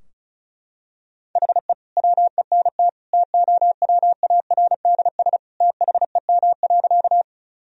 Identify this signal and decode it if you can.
Morse code, heard as a keyed tone, about 36 wpm, 700 Hz. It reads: HE WENT TOWARDS THEM.
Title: Sentences from Top 1000 Words - Review Part 2 - 35wpm